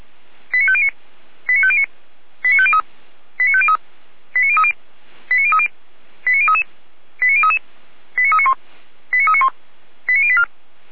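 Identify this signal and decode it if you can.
Five-tone selective call (ZVEI), heard as quick runs of five tones, tones of about 70 ms. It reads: 89509 89509 89503 89503 89030 89030 89033 89033 89391 89391 89084
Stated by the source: ZVEI five-tone selective call 89042, received by SDR.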